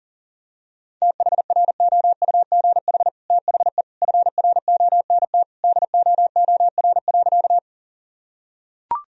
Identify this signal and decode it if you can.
Morse code, heard as a keyed tone, about 40 wpm, 700 Hz. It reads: THROUGH THE FRONT DOOR.
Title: Sentences from Top 1000 Words - Review Part 2 - 40wpm